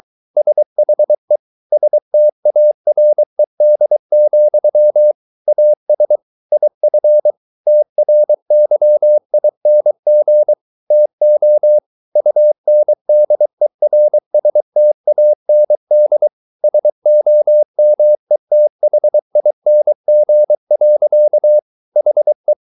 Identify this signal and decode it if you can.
Morse code, heard as a keyed tone, about 23 wpm, 600 Hz. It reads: SHE STARED, AS IF TRYING TO UNDERSTAND SOMETHING. HE